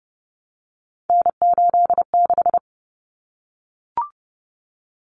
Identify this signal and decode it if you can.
Morse code, heard as a keyed tone, about 30 wpm, 700 Hz. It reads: N86
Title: Common Cut Numbers - 3-Digit 30wpm